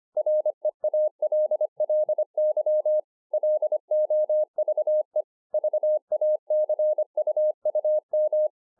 Morse code, heard as a keyed tone, about 25 wpm, 610 Hz. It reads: REALLY LOVE VACUUM